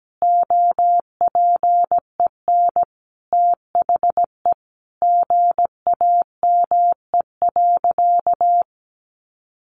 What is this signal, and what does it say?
Morse code, heard as a keyed tone, about 17 wpm, 700 Hz. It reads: OPEN THE GAME.
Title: Sentences from Top 1500 Words 17wpm